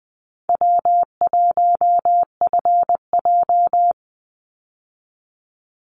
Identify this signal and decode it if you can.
Morse code, heard as a keyed tone, about 20 wpm, 700 Hz. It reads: W1FJ
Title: Easy Call Signs Repeated 2x - 20wpm